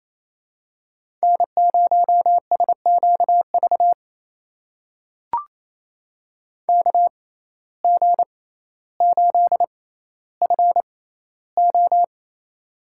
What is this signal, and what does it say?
Morse code, heard as a keyed tone, about 28 wpm, spaced as for 14 wpm, 700 Hz. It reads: N0SQV KG8FO
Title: USA Call Signs - 5 Characters - 6x Character Spacing - 28wpm